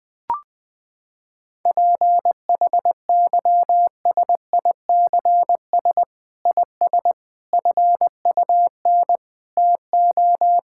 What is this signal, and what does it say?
Morse code, heard as a keyed tone, about 21 wpm, 700 Hz. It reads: PHYSICS IS FUN TO
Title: Sentences from Top 1000 Words - Review Part 5 - 20wpm